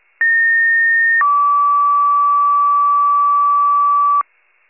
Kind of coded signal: two-tone page

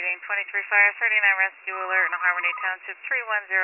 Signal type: Emergency Alert System (SAME) tones

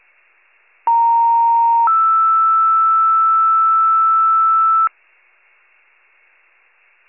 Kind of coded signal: two-tone page

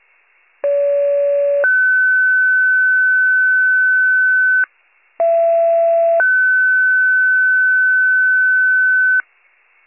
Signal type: two-tone page

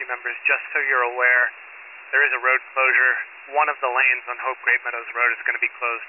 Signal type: Emergency Alert System (SAME) tones

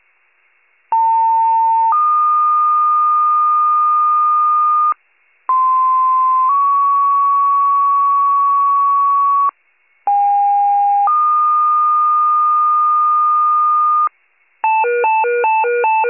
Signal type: two-tone page

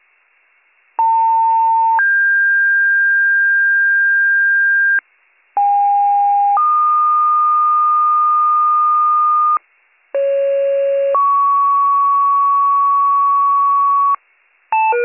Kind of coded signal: two-tone page

